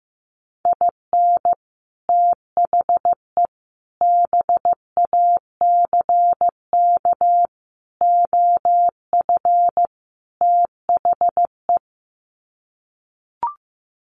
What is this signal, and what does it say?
Morse code, heard as a keyed tone, about 15 wpm, 700 Hz. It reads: IN THE BACK OF THE
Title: Sets of 5 Words 15wpm